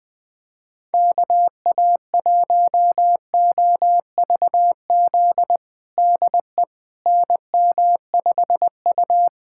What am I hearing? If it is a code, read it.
Morse code, heard as a keyed tone, about 20 wpm, 700 Hz. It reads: KA1OVZ DE NM5U
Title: USA Callsign DE Callsign 20wpm